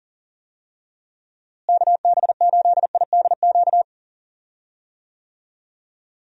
Morse code, heard as a keyed tone, about 39 wpm, 700 Hz. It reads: KB8IDQ
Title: USA Call Signs - 6 Characters - Repeated 3x - 40wpm